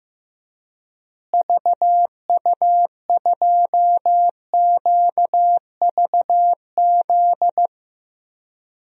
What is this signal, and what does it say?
Morse code, heard as a keyed tone, about 15 wpm, 700 Hz. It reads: VU2QVZ
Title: DXCC-100 Call Signs - 6 Characters - Repeated 3x - 15wpm